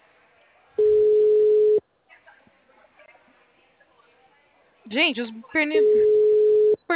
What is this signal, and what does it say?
Ringback tone (Europe, 425 Hz), with speech.